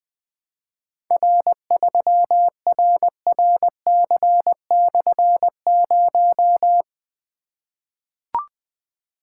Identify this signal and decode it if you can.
Morse code, heard as a keyed tone, about 20 wpm, 700 Hz. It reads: R3RRC/0
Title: Difficult Call Signs 20wpm